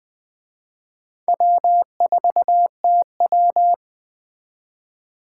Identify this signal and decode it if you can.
Morse code, heard as a keyed tone, about 20 wpm, 700 Hz. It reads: W4TW
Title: Speed Racing - USA 1x2 Call Signs 17wpm